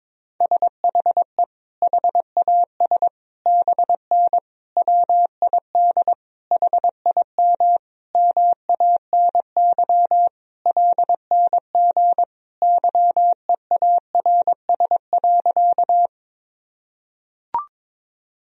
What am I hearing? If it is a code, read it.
Morse code, heard as a keyed tone, about 22 wpm, 700 Hz. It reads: SHE HAS BN WID HIM MANY LNG YEARS.
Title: Sentences from Top 200 Words and Abbreviations 22wpm